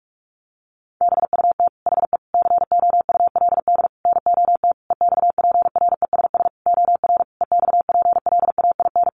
Morse code, heard as a keyed tone, about 44 wpm, 700 Hz. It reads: BUT HE COULD NOT EXPRESS OR EXPLAIN
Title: Sentences from Top 1100 Words 45wpm